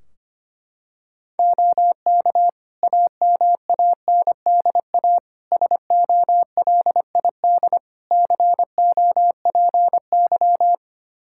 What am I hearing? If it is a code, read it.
Morse code, heard as a keyed tone, about 25 wpm, 700 Hz. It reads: OK AMANDA SOLID COPY
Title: K1USN SST QSO Practice 25wpm